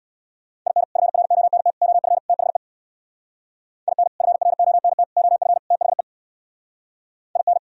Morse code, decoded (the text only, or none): IN FRONT OF THE IN FRONT OF THE IN